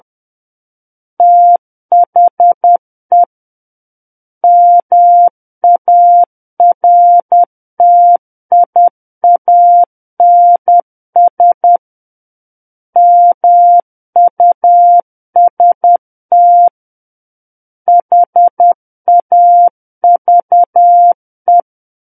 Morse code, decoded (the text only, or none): THE MARTIANS MUST HAVE